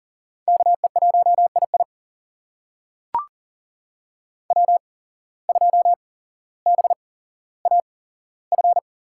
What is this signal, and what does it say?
Morse code, heard as a keyed tone, about 39 wpm, spaced as for 17 wpm, 700 Hz. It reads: KE1II W2BAF
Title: USA Call Signs - 5 Characters - 8x Character Spacing - 40wpm